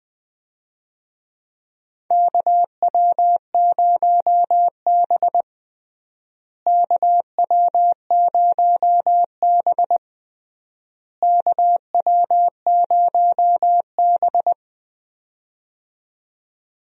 Morse code, heard as a keyed tone, about 20 wpm, 700 Hz. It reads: KW0B KW0B KW0B
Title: USA Call Signs - 4 Characters - Repeated 3x - 20wpm